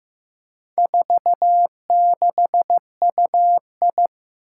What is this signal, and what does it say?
Morse code, heard as a keyed tone, about 15 wpm, 700 Hz. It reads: 46UI
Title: Four Challenging Letter-Number Combinations 15wpm